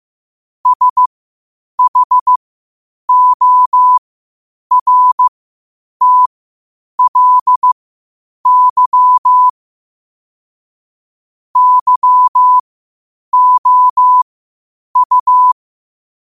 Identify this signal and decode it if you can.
Morse code, heard as a keyed tone, about 15 wpm, spaced as for 8 wpm, 1000 Hz. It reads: SHORTLY YOU